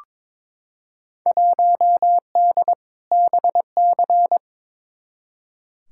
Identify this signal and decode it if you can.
Morse code, heard as a keyed tone, about 22 wpm, 700 Hz. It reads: 1D BC
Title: Field Day Exchanges - 22wpm